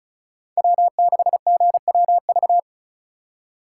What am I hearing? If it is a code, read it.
Morse code, heard as a keyed tone, about 43 wpm, 700 Hz. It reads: W6GWV
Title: Speed Racing - USA 1x3 Call Signs 35wpm